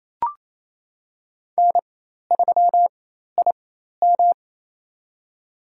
Morse code, heard as a keyed tone, about 28 wpm, spaced as for 13 wpm, 700 Hz. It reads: N3IM
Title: USA Call Signs - 4 Characters - 4x Character Spacing - 28wpm